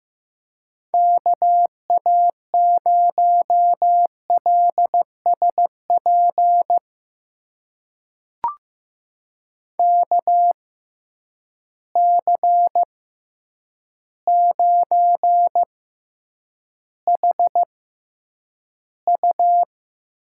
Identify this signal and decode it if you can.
Morse code, heard as a keyed tone, about 15 wpm, spaced as for 8 wpm, 700 Hz. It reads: KA0LSP KC9HU
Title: USA Call Signs - 6 Characters - 6x Character Spacing - 15wpm